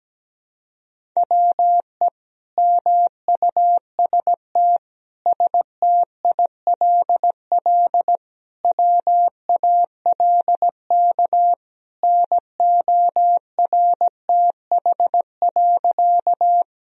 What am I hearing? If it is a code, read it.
Morse code, heard as a keyed tone, about 17 wpm, 700 Hz. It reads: WE MUST STILL WALK NORTH.
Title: Sentences from Top 1000 Words - Review Part 1 - 17wpm